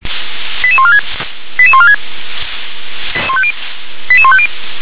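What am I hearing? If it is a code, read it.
Five-tone selective call (ZVEI), heard as quick runs of five tones, tones of about 70 ms. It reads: 88146 88146 88150
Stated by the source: ZVEI five-tone selective call 88154 received on an SDR.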